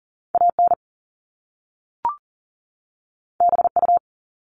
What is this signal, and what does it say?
Morse code, heard as a keyed tone, about 40 wpm, 700 Hz. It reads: AN BU
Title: Top Two Letter Combinations in Words 40wpm